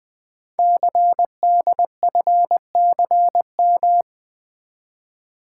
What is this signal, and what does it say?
Morse code, heard as a keyed tone, about 20 wpm, 700 Hz. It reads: CDFCM
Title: Code Groups - 5 Letters - 20wpm